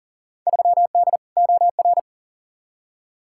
Morse code, heard as a keyed tone, about 40 wpm, 700 Hz. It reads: 3D OR
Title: Field Day Exchanges - 40wpm